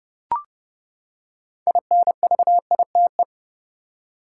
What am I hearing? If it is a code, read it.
Morse code, heard as a keyed tone, about 31 wpm, 700 Hz. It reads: INVITE
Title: Six Letter Words - 30wpm